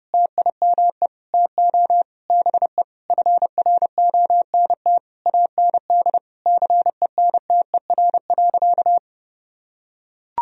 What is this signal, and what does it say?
Morse code, heard as a keyed tone, about 30 wpm, 700 Hz. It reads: TIME TO BE FRONT AND CENTER.
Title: Sentences from Top 1000 Words - Review Part 2 - 30wpm